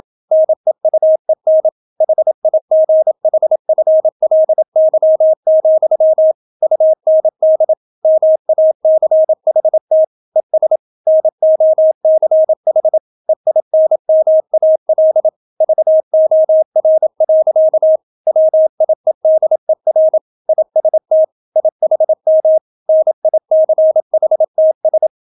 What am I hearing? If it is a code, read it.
Morse code, heard as a keyed tone, about 27 wpm, 630 Hz. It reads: NEUEN HIGHFLY, UND MACHT ES NOCH EINMAL VOR. WIEDER IST IHM NICHTS